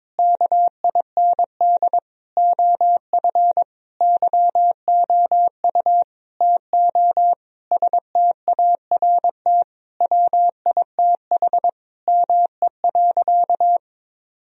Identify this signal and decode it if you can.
Morse code, heard as a keyed tone, about 22 wpm, 700 Hz. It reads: KIND OF YOU TO START WITH ME.
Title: Sentences from Top 1000 Words - Review Part 1 - 22wpm